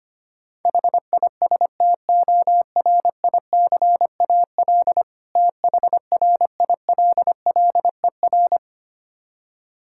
Morse code, heard as a keyed tone, about 25 wpm, 700 Hz. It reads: HISTORICAL THRILLER
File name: practice / 25wpm / Difficult Words with Hints - 25wpm